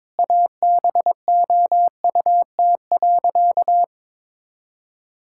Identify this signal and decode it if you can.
Morse code, heard as a keyed tone, about 21 wpm, 700 Hz. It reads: ABOUT.
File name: practice / 22wpm / Sentences with 8-Word Phrases - 22wpm